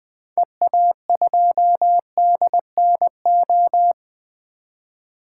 Morse code, heard as a keyed tone, about 21 wpm, 700 Hz. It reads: EA2DNO
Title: 2023 SOTA Activator Call Signs - 20wpm